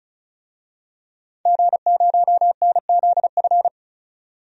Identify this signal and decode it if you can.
Morse code, heard as a keyed tone, about 35 wpm, 700 Hz. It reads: G0NZF
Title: DXCC-100 Call Signs - 5 Characters - 35wpm